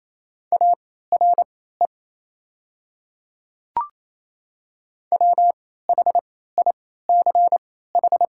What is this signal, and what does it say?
Morse code, heard as a keyed tone, about 28 wpm, spaced as for 16 wpm, 700 Hz. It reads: ARE WHICH
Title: 3x Character Spacing - Top 300 Words - 28wpm